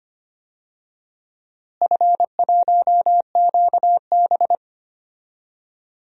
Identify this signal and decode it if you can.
Morse code, heard as a keyed tone, about 25 wpm, 700 Hz. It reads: F1QB
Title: DXCC-100 Call Signs - 4 Characters - Repeated 2x - 25wpm